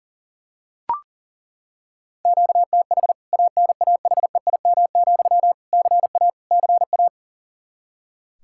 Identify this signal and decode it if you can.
Morse code, heard as a keyed tone, about 40 wpm, 700 Hz. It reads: QTH ANAHEIM, CA CA
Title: K1USN SST QSO Practice 40wpm